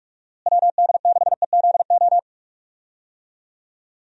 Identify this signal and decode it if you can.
Morse code, heard as a keyed tone, about 46 wpm, 700 Hz. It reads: WD6EZO
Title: USA Call Signs - 6 Characters - Repeated 3x - 45wpm